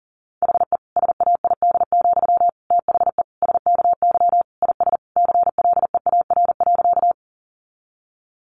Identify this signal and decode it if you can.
Morse code, heard as a keyed tone, about 40 wpm, 700 Hz. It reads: HE SAID, THE SKY IS CLEAR.
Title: Sentences from Top 1200 Words 40wpm